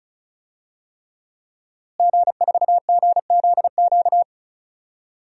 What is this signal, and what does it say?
Morse code, heard as a keyed tone, about 36 wpm, 700 Hz. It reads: G4GZQ
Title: DXCC-100 Call Signs - 5 Characters - Repeated 3x - 35wpm